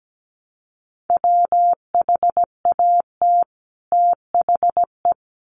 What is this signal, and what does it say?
Morse code, heard as a keyed tone, about 17 wpm, 700 Hz. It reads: WHAT THE
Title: Sets of 2 Words 17wpm